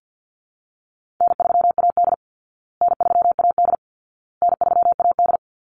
Morse code, heard as a keyed tone, about 50 wpm, 700 Hz. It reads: N3AD N3AD N3AD
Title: Call Signs Repeated 2x - 50wpm